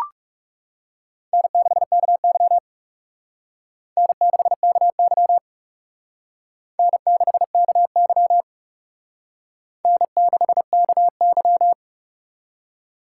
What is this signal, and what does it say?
Morse code, heard as a keyed tone, about 36 wpm, 700 Hz. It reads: N6KY N6KY N6KY N6KY
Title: Speed Racing - USA Call Signs 30wpm